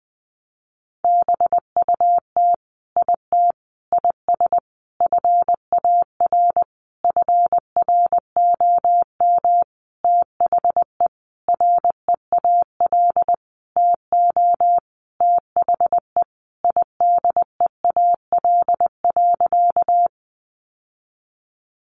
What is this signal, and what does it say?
Morse code, heard as a keyed tone, about 20 wpm, 700 Hz. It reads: BUT IT IS FAR FROM THE REAL TO THE IDEAL.
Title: Sentences from Top 1400 Words 20wpm